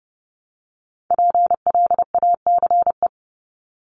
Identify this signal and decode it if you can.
Morse code, heard as a keyed tone, about 30 wpm, 700 Hz. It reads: PLACE
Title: Top 500 Words 30wpm